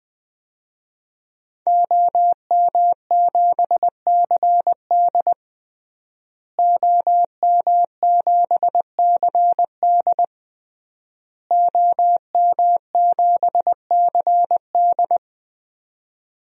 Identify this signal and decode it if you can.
Morse code, heard as a keyed tone, about 21 wpm, 700 Hz. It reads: OM7CD OM7CD OM7CD
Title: DXCC-100 Call Signs - 5 Characters - Repeated 3x - 20wpm